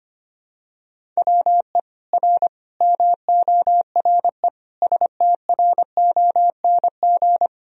Morse code, heard as a keyed tone, about 25 wpm, 700 Hz. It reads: WE R MORE STRONG
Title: Sentences from Top 500 Words and Abbreviations 25wpm